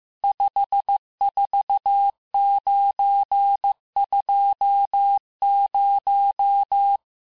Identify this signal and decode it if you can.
Morse code, heard as a keyed tone, about 15 wpm, 800 Hz. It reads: 54920
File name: cours-du-lundi files / cours-1585-texte-chiffre-vitesse-14mpm